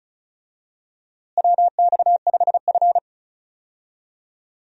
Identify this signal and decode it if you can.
Morse code, heard as a keyed tone, about 36 wpm, 700 Hz. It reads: WX5F
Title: USA Call Signs - 4 Characters - Repeated 3x - 35wpm